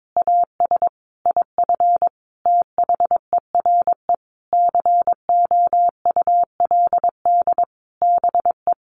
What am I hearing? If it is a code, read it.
Morse code, heard as a keyed tone, about 22 wpm, 700 Hz. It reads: AS IF THERE COULD BE
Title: A Sound Of Thunder 22wpm